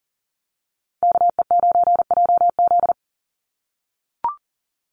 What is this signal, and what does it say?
Morse code, heard as a keyed tone, about 40 wpm, 700 Hz. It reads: KE9JZ
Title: USA Call Signs 40wpm